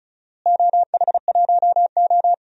Morse code, heard as a keyed tone, about 34 wpm, 700 Hz. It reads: OH1O
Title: DXCC-100 Call Signs - 4 Characters - Repeated 2x - 35wpm